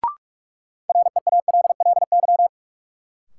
Morse code, heard as a keyed tone, about 46 wpm, 700 Hz. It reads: REALLY
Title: Sentences from Top 600 Words - New Word Drill - 45wpm